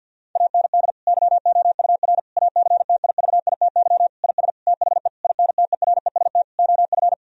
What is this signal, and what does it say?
Morse code, heard as a keyed tone, about 50 wpm, 700 Hz. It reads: AND YOUR ACTIVITY IS THE INTEREST OF
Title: Sentences from Top 1000 Words - Review Part 4 - 50wpm